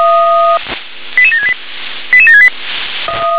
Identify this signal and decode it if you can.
Five-tone selective call (ZVEI), heard as quick runs of five tones, tones of about 80 ms. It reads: 88768 88768
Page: SDR/ZVEI